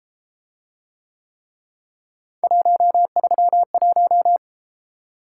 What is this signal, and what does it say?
Morse code, heard as a keyed tone, about 31 wpm, 700 Hz. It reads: 131